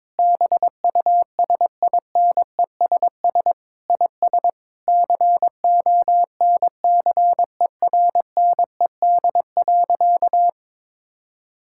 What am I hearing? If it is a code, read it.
Morse code, heard as a keyed tone, about 22 wpm, 700 Hz. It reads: BUSINESS IS CONCERNED.
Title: Sentences from Top 1000 Words - Review Part 6 - 22wpm